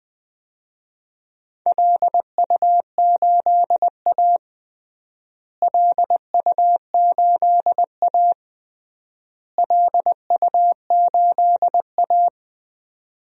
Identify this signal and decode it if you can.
Morse code, heard as a keyed tone, about 21 wpm, 700 Hz. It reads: LU8A LU8A LU8A
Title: DXCC-100 Call Signs - 4 Characters - Repeated 3x - 20wpm